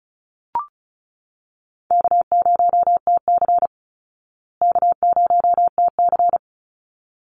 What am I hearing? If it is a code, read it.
Morse code, heard as a keyed tone, about 34 wpm, 700 Hz. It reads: K0TC K0TC
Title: Call Signs Repeated 3x - 35wpm